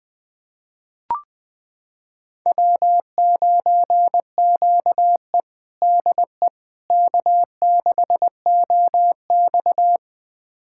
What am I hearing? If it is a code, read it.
Morse code, heard as a keyed tone, about 20 wpm, 700 Hz. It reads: W9QE DE K6OX
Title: USA Callsign DE Callsign - 4 Characters - 20wpm